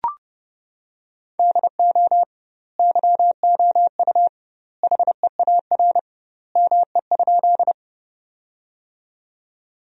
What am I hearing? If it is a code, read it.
Morse code, heard as a keyed tone, about 30 wpm, 700 Hz. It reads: DO YOU HEAR ME?
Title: 2x Word-Spacing - Sentences from Top 300 Words - 30wpm